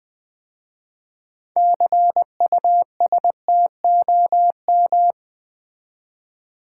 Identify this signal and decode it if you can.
Morse code, heard as a keyed tone, about 20 wpm, 700 Hz. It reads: CUSTOM
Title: 6 Letter Words - First 3 Letters Given - 20wpm